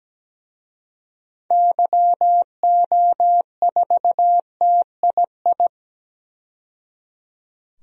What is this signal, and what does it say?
Morse code, heard as a keyed tone, about 17 wpm, 700 Hz. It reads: YO4TII
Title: DXCC-100 Call Signs - 6 Characters - 17wpm